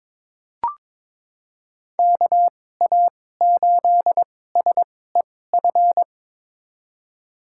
Morse code, heard as a keyed tone, about 22 wpm, spaced as for 16 wpm, 700 Hz. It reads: KA8SEF
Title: USA Call Signs - 2x Character Spacing - 22wpm